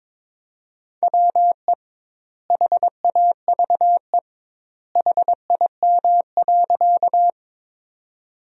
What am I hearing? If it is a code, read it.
Morse code, heard as a keyed tone, about 22 wpm, 700 Hz. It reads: WE HAVE HIM.
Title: 2x Word-Spacing - Sentences from Top 100 Words - 22wpm